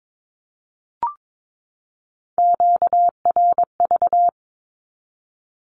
Morse code, heard as a keyed tone, about 22 wpm, 700 Hz. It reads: QRV
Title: Q-Codes 22wpm